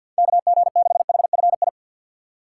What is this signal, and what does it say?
Morse code, heard as a keyed tone, about 51 wpm, 700 Hz. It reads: KG6HFI
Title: USA Call Signs - 3x Character Spacing - 50wpm